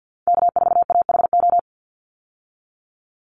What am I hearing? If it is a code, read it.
Morse code, heard as a keyed tone, about 50 wpm, 700 Hz. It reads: K4AHO
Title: Call Signs Repeated 3x - 50wpm